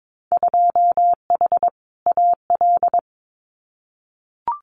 Morse code, heard as a keyed tone, about 22 wpm, 700 Hz. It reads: 2H AL
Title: Winter Field Day Exchanges 22wpm